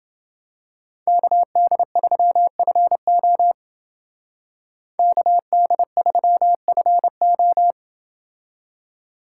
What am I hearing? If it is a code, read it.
Morse code, heard as a keyed tone, about 29 wpm, 700 Hz. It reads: KD3FO KD3FO